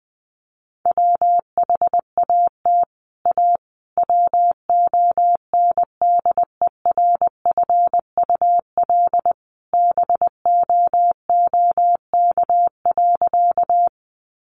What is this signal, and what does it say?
Morse code, heard as a keyed tone, about 20 wpm, 700 Hz. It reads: WHAT A WONDERFUL BOOK.
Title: Sentences from Top 1100 Words 20wpm